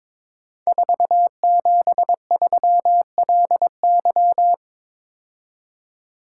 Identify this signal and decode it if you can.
Morse code, heard as a keyed tone, about 23 wpm, 700 Hz. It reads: 473LY
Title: Code Groups - 5 Letter:Numbers - 22wpm